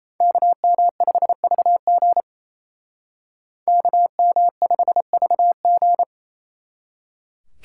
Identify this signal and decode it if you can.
Morse code, heard as a keyed tone, about 30 wpm, 700 Hz. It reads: KM5VG KM5VG
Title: Speed Racing - USA 2x2 Call Signs 28wpm